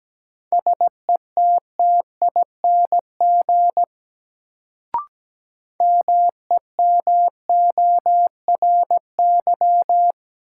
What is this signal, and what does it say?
Morse code, heard as a keyed tone, about 17 wpm, 700 Hz. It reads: SETTING MEMORY
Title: Sentences from Top 1300 Words - New Word Drill - 17wpm